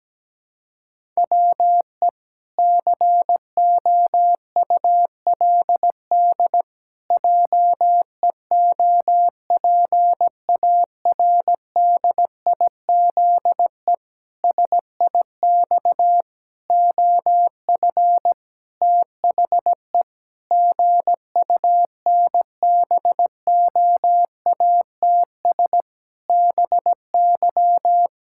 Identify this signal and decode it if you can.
Morse code, heard as a keyed tone, about 17 wpm, 700 Hz. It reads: WE COULD JEOPARDIZE SIX OF THE GUNBOATS BY
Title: Pangram Challenge v2 - 17wpm